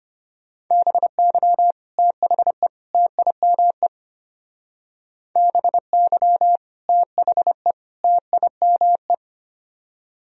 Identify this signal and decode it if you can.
Morse code, heard as a keyed tone, about 27 wpm, 700 Hz. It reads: BY THE TIME BY THE TIME
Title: Speed-Racing - Sets of 3 Words 25wpm